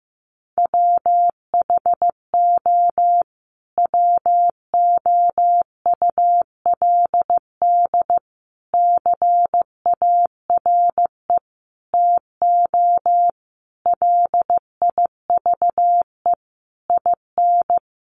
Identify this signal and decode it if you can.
Morse code, heard as a keyed tone, about 15 wpm, 700 Hz. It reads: WHO WOULD CARE TO LIVE IN
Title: Sentences from Top 1500 Words 15wpm